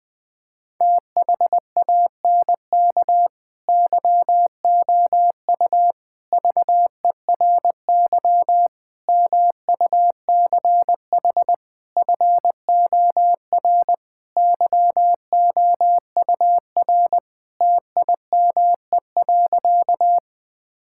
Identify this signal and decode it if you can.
Morse code, heard as a keyed tone, about 20 wpm, 700 Hz. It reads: THANK YOU VERY MUCH FOR YOUR TIME.
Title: Sentences with 5-Word Phrases - Spoken Beginning - 20wpm